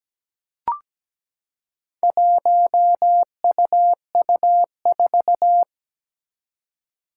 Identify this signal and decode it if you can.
Morse code, heard as a keyed tone, about 17 wpm, 700 Hz. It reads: 1UU4
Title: Four Challenging Letter-Number Combinations 17wpm